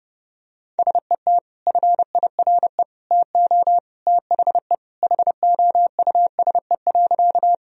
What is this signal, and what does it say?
Morse code, heard as a keyed tone, about 30 wpm, 700 Hz. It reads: SET FIRE TO THE HOUSE.
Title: Sentences from Top 1000 Words - Review Part 1 - 30wpm